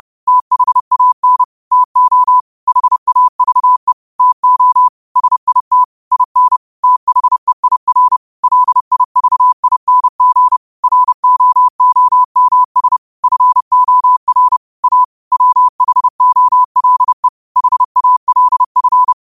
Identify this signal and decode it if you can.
Morse code, heard as a keyed tone, about 30 wpm, 1000 Hz. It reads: THAN TO HAVE TO SIT IN THEIR LIVING ROOMS FOR A WHOLE HALF
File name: qotdfiles / 30wpm